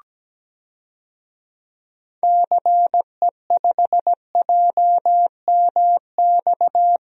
Morse code, heard as a keyed tone, about 17 wpm, 700 Hz. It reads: CE5JMX
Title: DXCC-100 Call Signs - 6 Characters - Repeated 2x - 17wpm